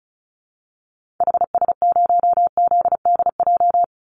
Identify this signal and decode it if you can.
Morse code, heard as a keyed tone, about 34 wpm, 700 Hz. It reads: HS0ZDJ
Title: Call Signs Repeated 3x - 35wpm